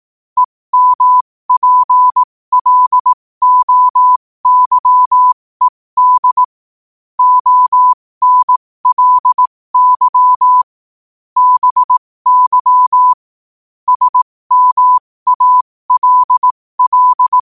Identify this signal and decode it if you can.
Morse code, heard as a keyed tone, about 18 wpm, spaced as for 15 wpm, 1000 Hz. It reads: EMPLOYED ONLY BY SMALL
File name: qotdfiles / 15wpm